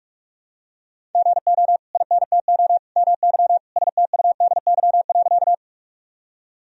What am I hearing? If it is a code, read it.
Morse code, heard as a keyed tone, about 46 wpm, 700 Hz. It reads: GO INTO MY STUDY.
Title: Sentences from Top 1000 Words - Review Part 1 - 45wpm